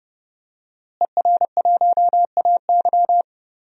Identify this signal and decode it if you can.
Morse code, heard as a keyed tone, about 30 wpm, 700 Hz. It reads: ER1AY
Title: DXCC-100 Call Signs - 5 Characters - Repeated 2x - 30wpm